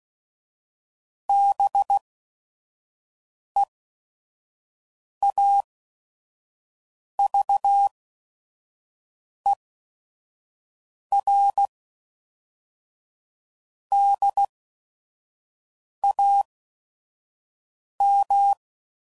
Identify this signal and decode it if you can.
Morse code, heard as a keyed tone, about 16 wpm, spaced as for 5 wpm, 800 Hz. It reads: BEAVERDAM
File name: CW Code Groups @5WPM / fromSM's / Cities